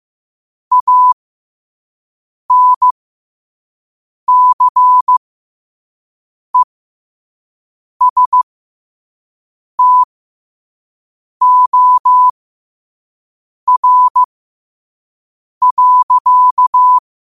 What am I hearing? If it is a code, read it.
Morse code, heard as a keyed tone, about 15 wpm, spaced as for 5 wpm, 1000 Hz. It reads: ANCESTOR.
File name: 5wpm